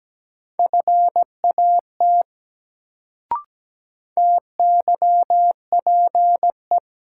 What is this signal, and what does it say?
Morse code, heard as a keyed tone, about 17 wpm, 700 Hz. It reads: FAT TYPE
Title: Sentences from Top 600 Words - New Word Drill - 17wpm